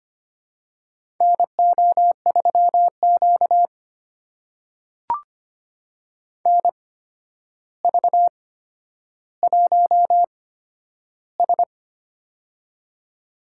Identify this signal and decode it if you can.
Morse code, heard as a keyed tone, about 25 wpm, spaced as for 11 wpm, 700 Hz. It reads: NO3Q NV1S
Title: USA Call Signs - 4 Characters - 8x Character Spacing - 25wpm